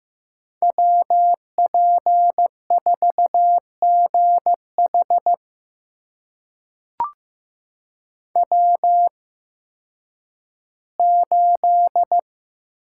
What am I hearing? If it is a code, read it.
Morse code, heard as a keyed tone, about 15 wpm, spaced as for 10 wpm, 700 Hz. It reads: WP4GH W8
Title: USA Call Signs - 8x Character Spacing - 15wpm